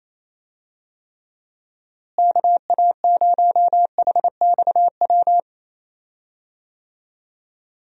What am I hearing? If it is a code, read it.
Morse code, heard as a keyed tone, about 27 wpm, 700 Hz. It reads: KA0HXW
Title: USA Call Signs - 6 Characters - Repeated 2x - 28wpm